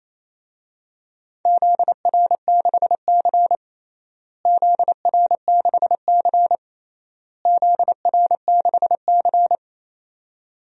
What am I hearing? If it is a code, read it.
Morse code, heard as a keyed tone, about 28 wpm, 700 Hz. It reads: ZR6C ZR6C ZR6C
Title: DXCC-100 Call Signs - 4 Characters - Repeated 3x - 28wpm